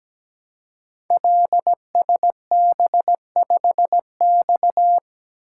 Morse code, heard as a keyed tone, about 17 wpm, 700 Hz. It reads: LSB5X
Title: Code Groups - 5 Letter:Numbers - 17wpm